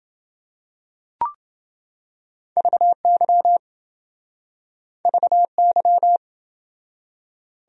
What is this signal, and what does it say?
Morse code, heard as a keyed tone, about 28 wpm, 700 Hz. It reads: VY VY